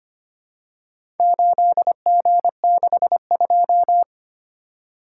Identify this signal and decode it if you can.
Morse code, heard as a keyed tone, about 25 wpm, 700 Hz. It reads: 8G62
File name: practice / 25wpm / Four Letter-Number Combinations - 25wpm